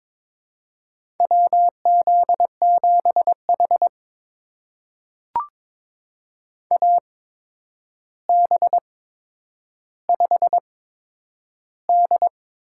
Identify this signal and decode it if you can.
Morse code, heard as a keyed tone, about 22 wpm, spaced as for 9 wpm, 700 Hz. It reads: WZ7H AB5D
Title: USA Call Signs - 4 Characters - 8x Character Spacing - 22wpm